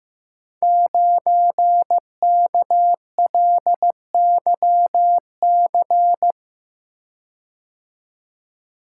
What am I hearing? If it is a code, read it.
Morse code, heard as a keyed tone, about 15 wpm, 700 Hz. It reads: 9KLYC
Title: Code Groups - 5 Letter:Numbers - 15wpm